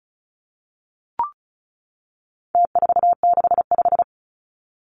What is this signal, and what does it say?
Morse code, heard as a keyed tone, about 34 wpm, 700 Hz. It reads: T465
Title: Numbers - Contest Style - 4-Digit 35wpm